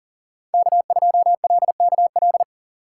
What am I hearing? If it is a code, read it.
Morse code, heard as a keyed tone, about 50 wpm, 700 Hz. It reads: K2LKL
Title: Speed Racing - USA 1x3 Call Signs 40wpm